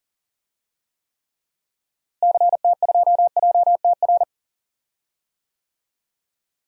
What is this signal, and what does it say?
Morse code, heard as a keyed tone, about 39 wpm, 700 Hz. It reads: CT2JTR
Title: DXCC-100 Call Signs - 6 Characters - Repeated 3x - 40wpm